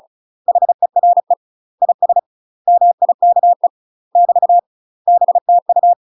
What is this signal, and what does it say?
Morse code, heard as a keyed tone, about 36 wpm, 700 Hz. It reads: HERE IS MIKE <BT> BTU
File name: UK MASTER QSOs TRAINING / 35WPM / UK MASTER QSOs 0020